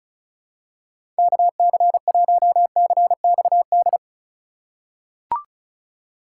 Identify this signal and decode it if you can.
Morse code, heard as a keyed tone, about 36 wpm, 700 Hz. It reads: KC1CXD
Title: USA Call Signs - 6 Characters - 3x Character Spacing - 35wpm